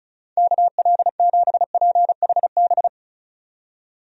Morse code, heard as a keyed tone, about 36 wpm, 700 Hz. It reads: KL7PHB
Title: DXCC-100 Call Signs - 6 Characters - 35wpm